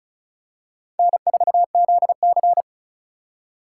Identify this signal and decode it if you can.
Morse code, heard as a keyed tone, about 36 wpm, 700 Hz. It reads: N4ZC
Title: USA Call Signs - 4 Characters - 3x Character Spacing - 35wpm